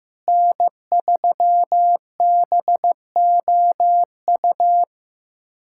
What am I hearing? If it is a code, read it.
Morse code, heard as a keyed tone, about 15 wpm, 700 Hz. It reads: N3BOU
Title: USA Callsign DE Callsign - 5 Characters - 15wpm